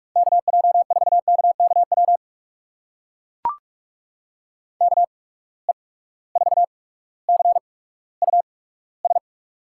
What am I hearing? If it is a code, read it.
Morse code, heard as a keyed tone, about 44 wpm, spaced as for 19 wpm, 700 Hz. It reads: KJ4KKW KE4CUS
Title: USA Call Signs - 6 Characters - 8x Character Spacing - 45wpm